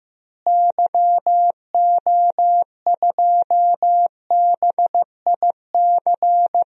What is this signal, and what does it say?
Morse code, heard as a keyed tone, about 15 wpm, 700 Hz. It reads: YO2BIC
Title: DXCC-100 Call Signs - 6 Characters - Repeated 3x - 15wpm